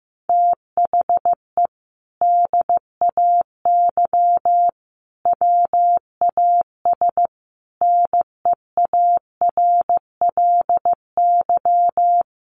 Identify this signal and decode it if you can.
Morse code, heard as a keyed tone, about 15 wpm, 700 Hz. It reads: THE DAY WAS NEARLY
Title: Sentences from Top 1300 Words 15wpm